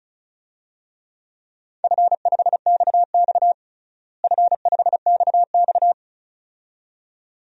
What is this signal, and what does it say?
Morse code, heard as a keyed tone, about 34 wpm, 700 Hz. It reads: F5XX F5XX
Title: DXCC-100 Call Signs - 4 Characters - Repeated 2x - 35wpm